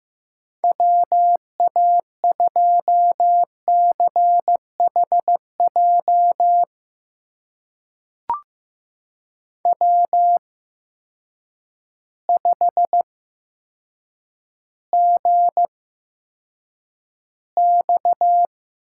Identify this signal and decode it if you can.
Morse code, heard as a keyed tone, about 15 wpm, spaced as for 8 wpm, 700 Hz. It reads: WA2CHJ W5GX
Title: USA Call Signs - 8x Character Spacing - 15wpm